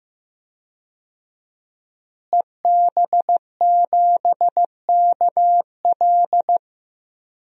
Morse code, heard as a keyed tone, about 15 wpm, 700 Hz. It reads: EB7KL
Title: DXCC-100 Call Signs - 5 Characters - Repeated 2x - 15wpm